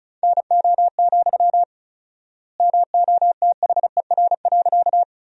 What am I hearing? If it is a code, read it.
Morse code, heard as a keyed tone, about 36 wpm, 700 Hz. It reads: NO, MOTHER.
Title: 4x Word-Spacing - Sentences from Top 200 Words - 35wpm